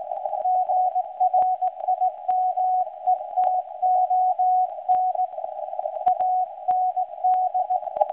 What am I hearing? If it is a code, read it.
Morse code, heard as a keyed tone, about 17 wpm, 740 Hz. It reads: GRIMETON AND
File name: WebSDR records / SAQ recording start 17.2kHz